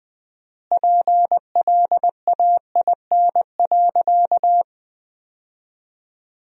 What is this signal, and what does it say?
Morse code, heard as a keyed tone, about 20 wpm, 700 Hz. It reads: PLAIN.
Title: 4x Word-Spacing - Sentences from Top 400 Words - 20wpm